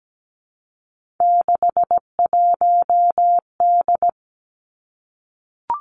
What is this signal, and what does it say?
Morse code, heard as a keyed tone, about 17 wpm, 700 Hz. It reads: 61D
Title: Three Challenging Letter-Number Combinations 17wpm